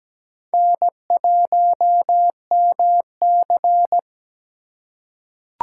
Morse code, heard as a keyed tone, about 17 wpm, 700 Hz. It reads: N1MC
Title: USA Call Signs - 4x Character Spacing - 17wpm